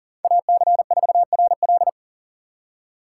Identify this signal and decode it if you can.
Morse code, heard as a keyed tone, about 49 wpm, 700 Hz. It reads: AC4RL
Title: Speed Racing - USA 2x2 Call Signs 40wpm